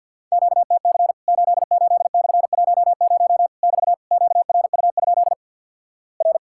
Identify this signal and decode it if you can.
Morse code, heard as a keyed tone, about 50 wpm, 700 Hz. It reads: QTC 88/10 <BT> QRU? R
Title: Worked All Europe DX Contest - QTC Exchange with Repeats - 50wpm